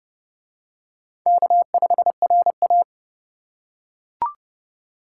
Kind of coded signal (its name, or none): Morse code